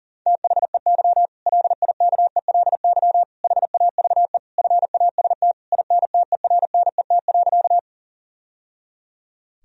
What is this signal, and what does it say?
Morse code, heard as a keyed tone, about 40 wpm, 700 Hz. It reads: THEY LIKELY HAVE FAST INTERNET.
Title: Sentences from Top 1000 Words - Review Part 4 - 40wpm